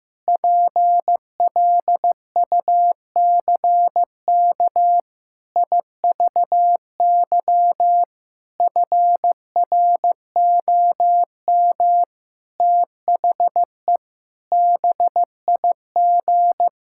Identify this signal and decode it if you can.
Morse code, heard as a keyed tone, about 16 wpm, 700 Hz. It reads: PLUCK IVY FROM THE BIG
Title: Pangram Challenge v2 - 15wpm